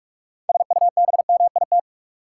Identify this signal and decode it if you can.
Morse code, heard as a keyed tone, about 46 wpm, 700 Hz. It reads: SUBMIT
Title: Six Letter Words - 45wpm